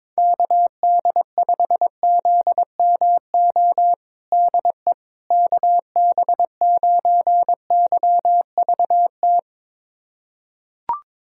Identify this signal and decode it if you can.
Morse code, heard as a keyed tone, about 22 wpm, 700 Hz. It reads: KD5ZMO DE KB9YVT
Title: USA Callsign DE Callsign - 6 Characters - 22wpm